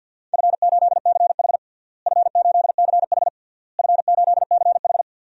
Morse code, heard as a keyed tone, about 52 wpm, 700 Hz. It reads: F8CH F8CH F8CH
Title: DXCC-100 Call Signs - 4 Characters - Repeated 3x - 50wpm